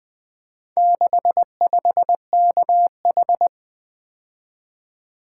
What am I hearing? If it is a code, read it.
Morse code, heard as a keyed tone, about 20 wpm, 700 Hz. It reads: 65KH
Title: Four Letter-Number Combinations - 20wpm